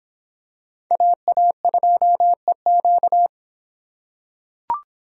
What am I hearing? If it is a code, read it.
Morse code, heard as a keyed tone, about 25 wpm, 700 Hz. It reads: AA2EQ